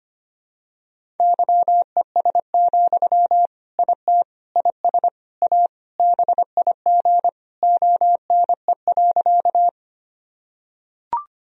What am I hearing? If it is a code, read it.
Morse code, heard as a keyed tone, about 25 wpm, 700 Hz. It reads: YES, IT IS A BIG ONE.
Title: Sentences from Top 1000 Words - Review Part 1 - 25wpm